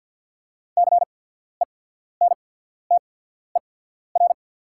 Morse code, heard as a keyed tone, about 48 wpm, spaced as for 14 wpm, 690 Hz. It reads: CENTER
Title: 8x Character Spacing - Top 500 Words - 50wpm